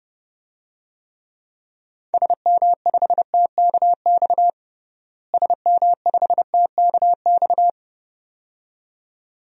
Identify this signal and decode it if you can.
Morse code, heard as a keyed tone, about 32 wpm, 700 Hz. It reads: SM5TKX SM5TKX
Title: DXCC-100 Call Signs - 6 Characters - Repeated 2x - 30wpm